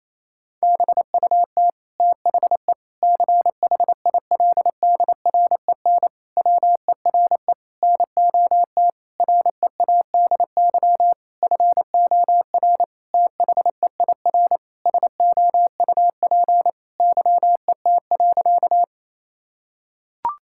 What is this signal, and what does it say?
Morse code, heard as a keyed tone, about 27 wpm, 700 Hz. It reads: BUT THE CHILDREN WERE NOT READY FOR THEIR SOUP YET.
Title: Sentences from Top 1000 Words - Review Part 6 - 28wpm